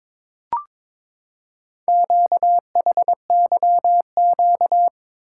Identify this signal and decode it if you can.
Morse code, heard as a keyed tone, about 23 wpm, 700 Hz. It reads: QHYQ
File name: practice / 22wpm / Four Letter-Number Combinations - 22wpm